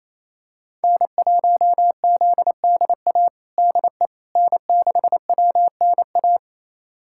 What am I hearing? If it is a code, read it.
Morse code, heard as a keyed tone, about 27 wpm, 700 Hz. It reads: N1ZDA DE N6WNA
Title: USA Callsign DE Callsign - 5 Characters - 28wpm